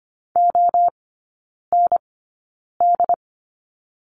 Morse code, heard as a keyed tone, about 25 wpm, spaced as for 9 wpm, 700 Hz. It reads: OND
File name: course / 25:10wpm / 093 Lesson 16e - Letter c - Three Characters 25wpm